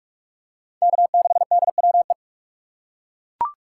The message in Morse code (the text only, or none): K6DWE